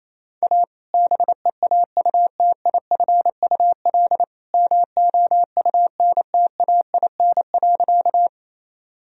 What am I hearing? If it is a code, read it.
Morse code, heard as a keyed tone, about 28 wpm, 700 Hz. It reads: A BEAUTIFUL MOUNTAIN.
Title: Sentences from Top 1000 Words - Review Part 1 - 28wpm